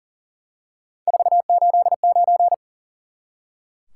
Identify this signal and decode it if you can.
Morse code, heard as a keyed tone, about 40 wpm, 700 Hz. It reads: 489
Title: Top 180 QSO Elements 40wpm